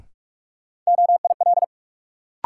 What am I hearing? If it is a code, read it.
Morse code, heard as a keyed tone, about 46 wpm, 700 Hz. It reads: OIL
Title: Sentences from Top 1000 Words - Review Part 4 - New Word Drill - 45wpm